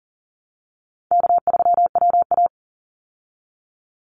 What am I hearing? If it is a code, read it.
Morse code, heard as a keyed tone, about 40 wpm, 700 Hz. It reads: K3WA
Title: Call Signs Repeated 3x - 40wpm